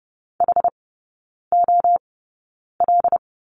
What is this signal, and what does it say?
Morse code, heard as a keyed tone, about 30 wpm, spaced as for 9 wpm, 700 Hz. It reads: HOL
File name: Lesson 16e - Letter c - Three Characters 30wpm